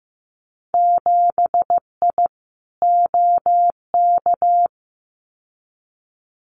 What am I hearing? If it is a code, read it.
Morse code, heard as a keyed tone, about 15 wpm, 700 Hz. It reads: 7I OK
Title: Winter Field Day Exchanges Encore 15wpm